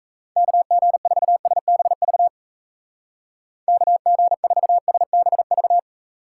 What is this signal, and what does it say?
Morse code, heard as a keyed tone, about 36 wpm, 700 Hz. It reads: KG4SBV KG4SBV